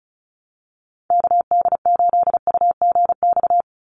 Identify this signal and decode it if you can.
Morse code, heard as a keyed tone, about 36 wpm, 700 Hz. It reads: KD8UGX
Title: USA Call Signs 35wpm